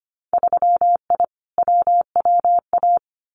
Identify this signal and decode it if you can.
Morse code, heard as a keyed tone, about 25 wpm, 700 Hz. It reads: 3I WWA
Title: Winter Field Day Exchanges Encore 25wpm